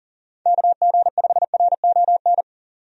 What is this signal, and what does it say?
Morse code, heard as a keyed tone, about 40 wpm, 700 Hz. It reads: KG5RON
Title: USA Call Signs - 6 Characters - 40wpm